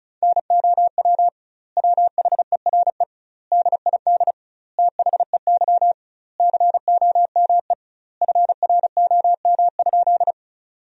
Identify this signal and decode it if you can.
Morse code, heard as a keyed tone, about 36 wpm, 700 Hz. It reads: NOW WHERE DID THEY COME FROM?
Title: 2x Word-Spacing - Sentences from Top 200 Words - 35wpm